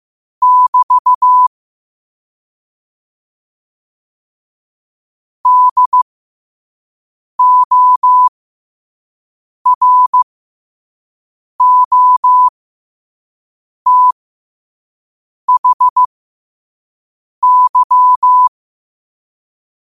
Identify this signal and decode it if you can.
Morse code, heard as a keyed tone, about 15 wpm, spaced as for 5 wpm, 1000 Hz. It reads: <BT> DOROTHY
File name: qotdfiles / 5wpm